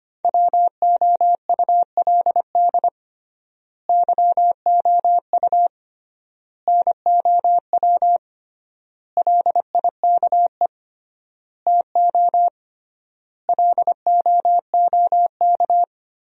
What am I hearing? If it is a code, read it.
Morse code, heard as a keyed tone, about 25 wpm, 700 Hz. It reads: WOULD YOU NOW LIKE TO LOOK
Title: 3x Word-Spacing - Sentences from Top 200 Words - 25wpm